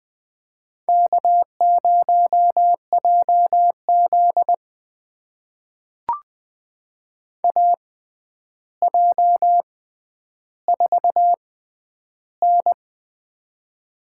Morse code, heard as a keyed tone, about 20 wpm, spaced as for 10 wpm, 700 Hz. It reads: K0JZ AJ4N
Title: USA Call Signs - 4 Characters - 6x Character Spacing - 20wpm